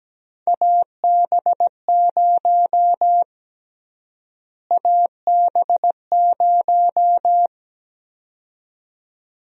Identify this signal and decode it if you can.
Morse code, heard as a keyed tone, about 17 wpm, 700 Hz. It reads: AB0 AB0
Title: USA Call Sign Prefixes - Repeated 2x - 17wpm